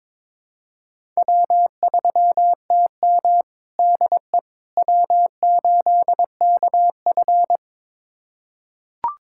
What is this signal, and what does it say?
Morse code, heard as a keyed tone, about 22 wpm, 700 Hz. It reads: W3TM DE W8KF
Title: USA Callsign DE Callsign - 4 Characters - 22wpm